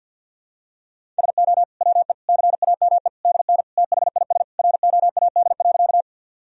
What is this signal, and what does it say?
Morse code, heard as a keyed tone, about 50 wpm, 700 Hz. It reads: SO WE CAME DN THIS ROAD.
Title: Sentences from Top 400 Words and Abbreviations 50wpm